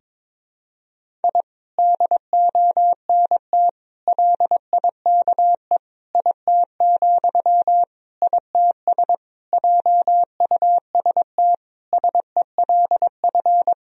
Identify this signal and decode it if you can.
Morse code, heard as a keyed tone, about 22 wpm, 700 Hz. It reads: I DONT LIKE IT, ITS JUST SELF
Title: Sentences from Top 1000 Words - Review Part 1 - 22wpm